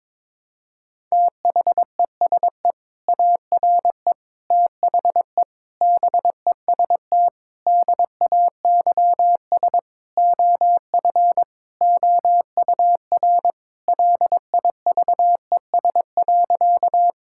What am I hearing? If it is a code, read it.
Morse code, heard as a keyed tone, about 22 wpm, 700 Hz. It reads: THESE ARE THE BEST DAYS OF OUR LIVES.
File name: practice / 22wpm / Sentences with 2-Word Phrases - 22wpm